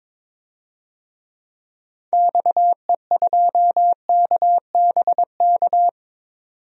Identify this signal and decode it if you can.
Morse code, heard as a keyed tone, about 22 wpm, 700 Hz. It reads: XE2KBK
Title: DXCC-100 Call Signs - 6 Characters - 22wpm